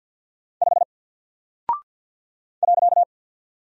Morse code, heard as a keyed tone, about 48 wpm, 700 Hz. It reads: 5 .
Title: Single Character (Letter:Number:Punctuation:Prosign) Alt 50wpm